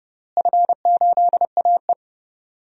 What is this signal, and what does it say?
Morse code, heard as a keyed tone, about 31 wpm, 700 Hz. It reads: F8AE
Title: DXCC-100 Call Signs - 4 Characters - Repeated 3x - 30wpm